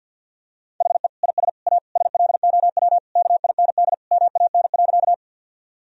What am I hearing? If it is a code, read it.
Morse code, heard as a keyed tone, about 46 wpm, 700 Hz. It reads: HE IS A SLOW KIND MAN.
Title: Sentences from Top 1000 Words - Review Part 2 - 50wpm